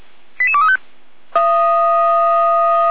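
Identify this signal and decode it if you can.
Five-tone selective call (ZVEI), heard as quick runs of five tones, tones of about 70 ms. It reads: 88235
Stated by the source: ZVEI five-tone selective call received on SDR.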